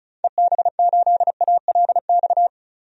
Morse code, heard as a keyed tone, about 36 wpm, 700 Hz. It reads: EB8ALX
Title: DXCC-100 Call Signs - 6 Characters - 35wpm